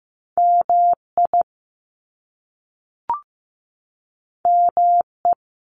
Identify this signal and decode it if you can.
Morse code, heard as a keyed tone, about 15 wpm, 700 Hz. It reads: MI ME
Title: US States 15wpm